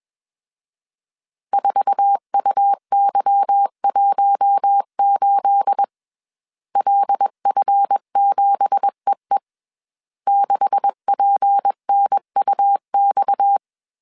Morse code, heard as a keyed tone, about 21 wpm, 800 Hz. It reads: 4UY18 LF7EE 6PNUX